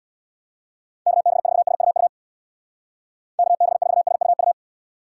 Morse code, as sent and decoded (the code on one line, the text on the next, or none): -.- -... ....- ... ..- ..- / -.- -... ....- ... ..- ..-
KB4SUU KB4SUU